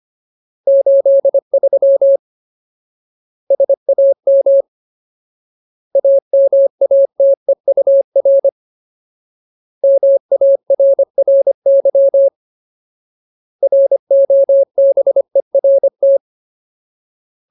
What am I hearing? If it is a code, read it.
Morse code, heard as a keyed tone, about 25 wpm, 550 Hz. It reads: 83 SAM AMATEUR MARRY ROBERT
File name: Ultimate Word List / 25WPM Ultimate